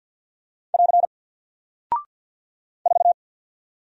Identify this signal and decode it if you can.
Morse code, heard as a keyed tone, about 48 wpm, 700 Hz. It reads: <AR> 4
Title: Single Character (Letter:Number:Punctuation:Prosign) Alt 50wpm